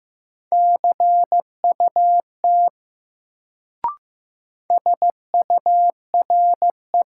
Morse code, heard as a keyed tone, about 15 wpm, 700 Hz. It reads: CUT SURE
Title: Sentences from Top 300 Words - New Word Drill - 15wpm